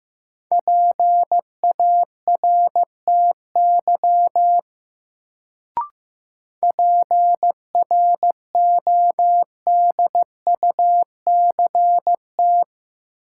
Morse code, heard as a keyed tone, about 15 wpm, 700 Hz. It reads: PARTY PRODUCT
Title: Sentences from Top 1400 Words - New Word Drill Alt - 15wpm